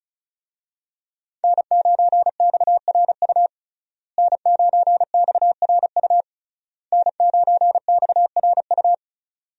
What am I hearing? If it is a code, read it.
Morse code, heard as a keyed tone, about 36 wpm, 700 Hz. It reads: N9XRU N9XRU N9XRU
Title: USA Call Signs - 5 Characters - Repeated 3x - 35wpm